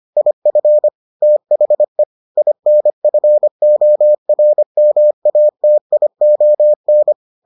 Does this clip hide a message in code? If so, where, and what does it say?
Yes: Morse code from 0.2 to 7.1 s, heard as a keyed tone, about 25 wpm, 600 Hz. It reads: IF THE INFORMATION